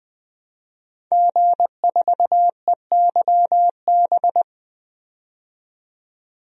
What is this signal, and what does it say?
Morse code, heard as a keyed tone, about 20 wpm, 700 Hz. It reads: G4EYB
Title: DXCC-100 Call Signs - 5 Characters - 20wpm